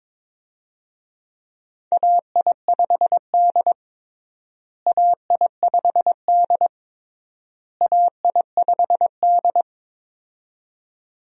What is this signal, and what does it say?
Morse code, heard as a keyed tone, about 22 wpm, 700 Hz. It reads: AI5D AI5D AI5D
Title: USA Call Signs - 4 Characters - Repeated 3x - 22wpm